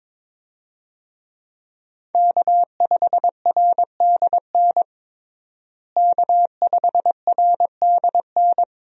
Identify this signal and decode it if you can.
Morse code, heard as a keyed tone, about 22 wpm, 700 Hz. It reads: K5RDN K5RDN
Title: USA Call Signs - 5 Characters - Repeated 2x - 22wpm